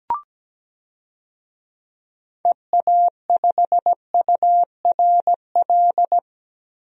Morse code, heard as a keyed tone, about 17 wpm, 700 Hz. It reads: EA5URL
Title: DXCC-100 Call Signs - 6 Characters - Repeated 2x - 17wpm